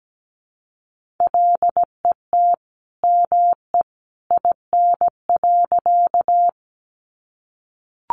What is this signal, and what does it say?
Morse code, heard as a keyed tone, about 17 wpm, 700 Hz. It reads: LET ME IN.
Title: Sentences from Top 300 Words 17wpm